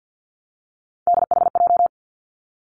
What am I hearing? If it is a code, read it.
Morse code, heard as a keyed tone, about 50 wpm, 700 Hz. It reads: DHJ
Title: Three Challenging Letter-Number Combinations 50wpm